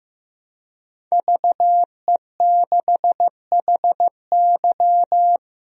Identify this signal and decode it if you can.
Morse code, heard as a keyed tone, about 15 wpm, 700 Hz. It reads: VE6HY
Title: DXCC-100 Call Signs - 5 Characters - 15wpm